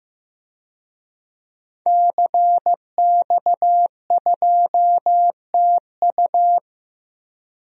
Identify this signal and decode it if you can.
Morse code, heard as a keyed tone, about 15 wpm, 700 Hz. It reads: CX2TU
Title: DXCC-100 Call Signs - 5 Characters - Repeated 2x - 15wpm